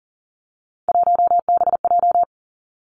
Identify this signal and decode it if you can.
Morse code, heard as a keyed tone, about 38 wpm, 690 Hz. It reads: 1BJ
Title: Three Challenging Letter-Number Combinations 40wpm